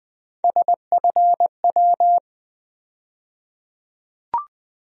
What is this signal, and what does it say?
Morse code, heard as a keyed tone, about 20 wpm, 700 Hz. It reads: SFW
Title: USA Call Sign Postfixes - 3 Letters 20wpm